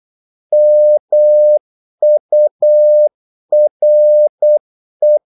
Morse code, heard as a keyed tone, about 8 wpm, 600 Hz. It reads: MURE
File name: Korte ord DK 8wpm